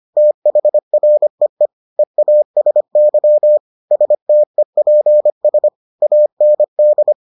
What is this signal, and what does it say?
Morse code, heard as a keyed tone, about 25 wpm, 600 Hz. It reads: THREE EASY STEPS AND